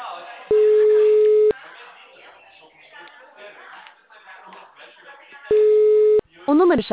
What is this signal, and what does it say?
Ringback tone (Europe, 425 Hz), with speech.